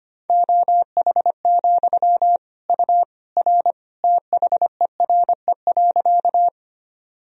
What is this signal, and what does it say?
Morse code, heard as a keyed tone, about 25 wpm, 700 Hz. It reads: OH, U R THERE.
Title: Sentences from Top 500 Words and Abbreviations 25wpm